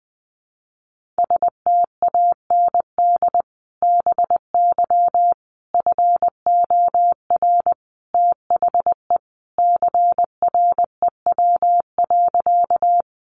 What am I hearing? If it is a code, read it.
Morse code, heard as a keyed tone, about 20 wpm, 700 Hz. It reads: STAND BY FOR THE CREW.
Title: Sentences from Top 1500 Words 20wpm